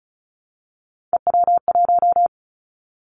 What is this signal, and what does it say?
Morse code, heard as a keyed tone, about 34 wpm, 700 Hz. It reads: EW1
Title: Callsign Prefixes for Top 100 DXCC - 35wpm